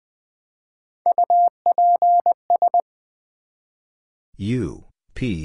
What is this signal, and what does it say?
Morse code, heard as a keyed tone, about 20 wpm, 700 Hz. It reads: UPS
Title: Build A Word - 1 Letter At A Time - 3 to 5-Letter Words - 20wpm